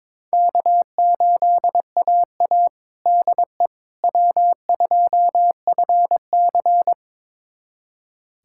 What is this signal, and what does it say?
Morse code, heard as a keyed tone, about 22 wpm, 700 Hz. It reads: K8AA DE W2FC
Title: USA Callsign DE Callsign - 4 Characters - 22wpm